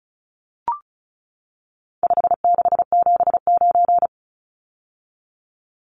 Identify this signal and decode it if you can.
Morse code, heard as a keyed tone, about 34 wpm, 700 Hz. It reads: 5679
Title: Numbers - 4-Digit 35wpm